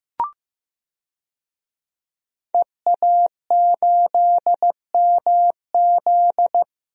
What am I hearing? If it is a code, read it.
Morse code, heard as a keyed tone, about 15 wpm, 700 Hz. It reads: EA8MZ
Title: DXCC-100 Call Signs - 5 Characters - 15wpm